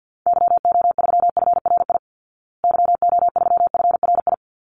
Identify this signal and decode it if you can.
Morse code, heard as a keyed tone, about 50 wpm, 700 Hz. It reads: YO3FRI YO3FRI
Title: Call Signs Repeated 3x - 50wpm